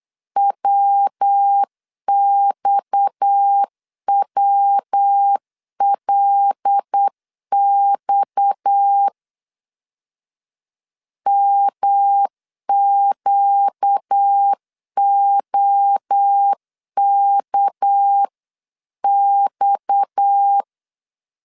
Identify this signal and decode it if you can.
Morse code, heard as a keyed tone, about 8 wpm, 800 Hz. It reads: WXWLX MQOKX
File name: LEZIONI / 6 PKX0 / 6 PKX 8wpm 3041